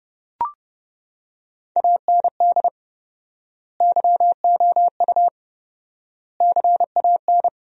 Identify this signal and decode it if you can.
Morse code, heard as a keyed tone, about 30 wpm, 700 Hz. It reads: AND YOU CAN
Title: 4x Word-Spacing - Sentences from Top 400 Words - 30wpm